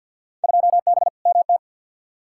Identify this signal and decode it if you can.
Morse code, heard as a keyed tone, about 47 wpm, 700 Hz. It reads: 2B MT
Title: Field Day Exchanges Encore - 50wpm